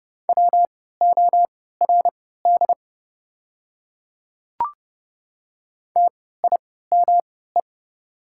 Morse code, heard as a keyed tone, about 30 wpm, spaced as for 17 wpm, 700 Hz. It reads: WORD TIME
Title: 3x Character Spacing - Top 100 Words - 30wpm